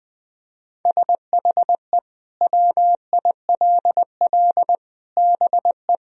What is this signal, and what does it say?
Morse code, heard as a keyed tone, about 20 wpm, 700 Hz. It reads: SHE WILL BE
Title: Sentences from Top 1000 Words - Review Part 1 - 20wpm